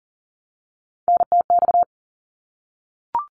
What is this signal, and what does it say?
Morse code, heard as a keyed tone, about 40 wpm, 700 Hz. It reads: NTX
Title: ARRL Sections 40wpm